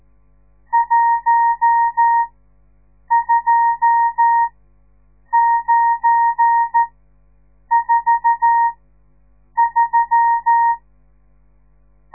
Morse code, heard as a keyed tone, about 13 wpm, spaced as for 7 wpm, 930 Hz. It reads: 12943
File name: radiomorningstar / morse / les 10